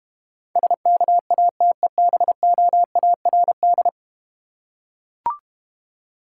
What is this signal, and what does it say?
Morse code, heard as a keyed tone, about 31 wpm, 700 Hz. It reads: SKATEBOARD
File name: Easy Words - Speed-Racing with Hints - 25wpm